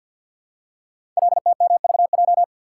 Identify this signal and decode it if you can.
Morse code, heard as a keyed tone, about 48 wpm, 700 Hz. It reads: LTMVJ
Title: Code Groups - 5 Letters - 50wpm